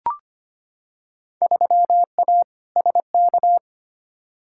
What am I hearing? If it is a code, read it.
Morse code, heard as a keyed tone, about 25 wpm, 700 Hz. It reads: 3A SK
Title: Field Day Exchanges Encore - 25wpm